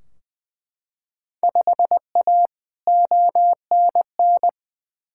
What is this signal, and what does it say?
Morse code, heard as a keyed tone, about 20 wpm, 700 Hz. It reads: 5A ONN
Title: Field Day Exchanges Encore - 20wpm